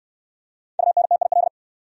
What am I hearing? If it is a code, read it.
Morse code, heard as a keyed tone, about 64 wpm, 700 Hz. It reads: UNTIL